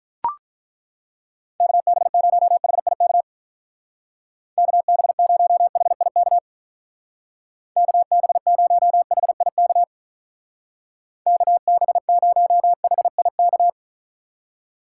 Morse code, heard as a keyed tone, about 42 wpm, 700 Hz. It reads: KB0HIK KB0HIK KB0HIK KB0HIK
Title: Speed Racing - USA 2x3 Call Signs 35wpm